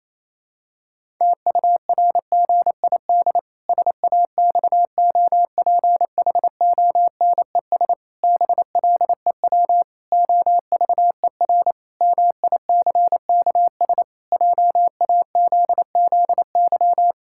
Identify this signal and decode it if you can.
Morse code, heard as a keyed tone, about 28 wpm, 700 Hz. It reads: TURGID SAXOPHONES BLEW OVER MICKS JAZZY
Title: Pangram Challenge v2 - 28wpm